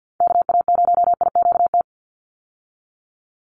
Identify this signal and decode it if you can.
Morse code, heard as a keyed tone, about 50 wpm, 700 Hz. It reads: KA0IQT
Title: Call Signs - 50wpm